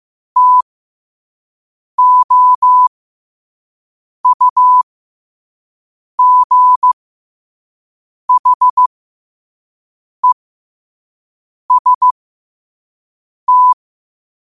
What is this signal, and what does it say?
Morse code, heard as a keyed tone, about 15 wpm, spaced as for 5 wpm, 1000 Hz. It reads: TOUGHEST